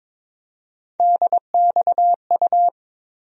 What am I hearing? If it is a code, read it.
Morse code, heard as a keyed tone, about 22 wpm, 700 Hz. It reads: DXU
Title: USA Call Sign Postfixes - 3 Letters 22wpm